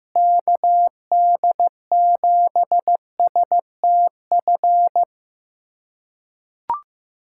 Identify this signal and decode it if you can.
Morse code, heard as a keyed tone, about 15 wpm, 700 Hz. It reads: KD7STF
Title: USA Callsign DE Callsign 15wpm